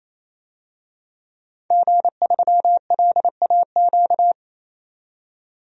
Morse code, heard as a keyed tone, about 28 wpm, 700 Hz. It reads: G3LAQ
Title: DXCC-100 Call Signs - 5 Characters - 28wpm